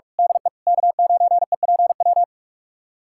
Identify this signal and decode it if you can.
Morse code, heard as a keyed tone, about 46 wpm, 700 Hz. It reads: DE K9EPW
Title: USA Callsign DE Callsign - 5 Characters - 45wpm